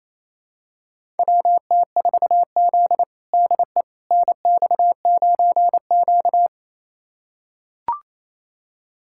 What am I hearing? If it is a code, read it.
Morse code, heard as a keyed tone, about 28 wpm, 700 Hz. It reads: WT4Z DE NX9Q
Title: USA Callsign DE Callsign - 4 Characters - 28wpm